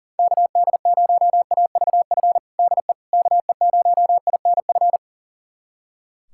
Morse code, heard as a keyed tone, about 40 wpm, 700 Hz. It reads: KD0AVF DE KE0INF
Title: USA Callsign DE Callsign 40wpm